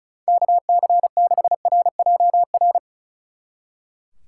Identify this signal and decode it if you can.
Morse code, heard as a keyed tone, about 46 wpm, 700 Hz. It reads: KC6RJR